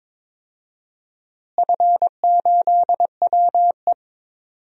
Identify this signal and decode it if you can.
Morse code, heard as a keyed tone, about 22 wpm, 700 Hz. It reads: F8WE
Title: DXCC-100 Call Signs - 4 Characters - Repeated 2x - 22wpm